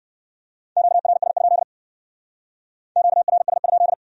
Morse code, heard as a keyed tone, about 68 wpm, 700 Hz. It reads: QRS? QRS?